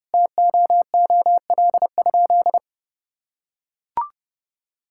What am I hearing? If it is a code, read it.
Morse code, heard as a keyed tone, about 32 wpm, 700 Hz. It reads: TOOL?
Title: Sentences from Top 1000 Words - Review Part 6 - 30wpm